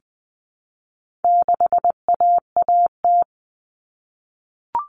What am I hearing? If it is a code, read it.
Morse code, heard as a keyed tone, about 20 wpm, 700 Hz. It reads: 6AAT
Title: Common Cut Numbers - 4-Digit 20wpm